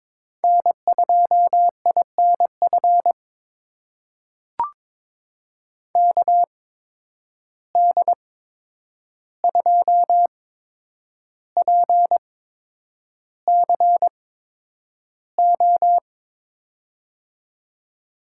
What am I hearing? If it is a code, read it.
Morse code, heard as a keyed tone, about 22 wpm, spaced as for 9 wpm, 700 Hz. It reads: N2INF KD2PCO
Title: USA Call Signs - 8x Character Spacing - 22wpm